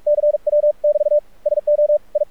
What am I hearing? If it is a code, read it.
Morse code, heard as a keyed tone, about 46 wpm, 590 Hz. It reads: CW <BT> SO I
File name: archived / CRSnet